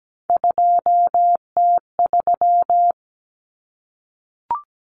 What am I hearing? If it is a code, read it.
Morse code, heard as a keyed tone, about 17 wpm, 700 Hz. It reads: 2T3
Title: Common Cut Numbers - 3-Digit 17wpm